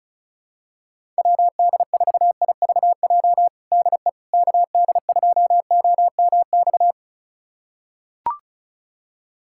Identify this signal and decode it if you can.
Morse code, heard as a keyed tone, about 36 wpm, 700 Hz. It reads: WD4IVJ DE KD2OMX
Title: USA Callsign DE Callsign 35wpm